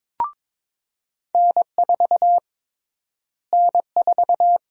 Morse code, heard as a keyed tone, about 22 wpm, 700 Hz. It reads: N4 N4
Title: USA Call Sign Prefixes - Repeated 2x - 22wpm